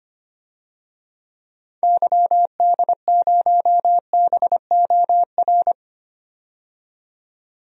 Morse code, heard as a keyed tone, about 25 wpm, 700 Hz. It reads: YD0BOR
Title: DXCC-100 Call Signs - 6 Characters - 25wpm